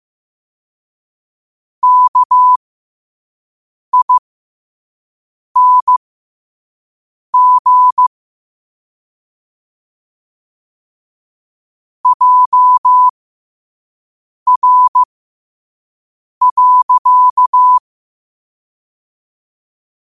Morse code, heard as a keyed tone, about 15 wpm, spaced as for 5 wpm, 1000 Hz. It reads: KING JR.
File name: qotdfiles / 5wpm